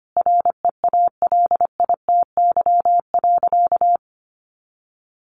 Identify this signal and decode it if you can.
Morse code, heard as a keyed tone, about 25 wpm, 700 Hz. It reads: REALITY.
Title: Sentences from Top 1200 Words 25wpm